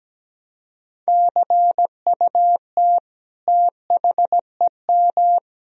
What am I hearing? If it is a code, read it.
Morse code, heard as a keyed tone, about 17 wpm, 700 Hz. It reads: CUT THEM
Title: Sentences from Top 1000 Words - Review Part 1 - 17wpm